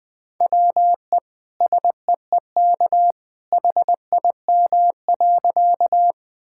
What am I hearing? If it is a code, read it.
Morse code, heard as a keyed tone, about 20 wpm, 700 Hz. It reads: WE SEEK HIM.
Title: Sentences from Top 1600 Words 20wpm